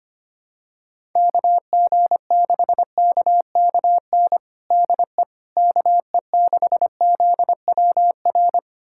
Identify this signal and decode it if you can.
Morse code, heard as a keyed tone, about 25 wpm, 700 Hz. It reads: KG6KKN DE KE6ZWR
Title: USA Callsign DE Callsign - 6 Characters - 25wpm